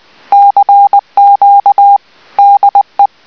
Morse code, heard as a keyed tone, about 20 wpm, 800 Hz. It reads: CQ DE